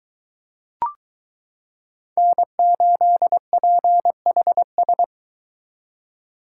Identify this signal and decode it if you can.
Morse code, heard as a keyed tone, about 23 wpm, 700 Hz. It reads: N8PHS